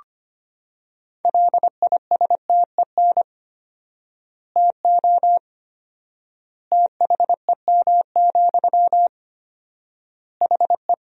Morse code, heard as a keyed tone, about 25 wpm, 700 Hz. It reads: LISTEN TO THEM, HE
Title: 4x Word-Spacing - Sentences from Top 400 Words - 25wpm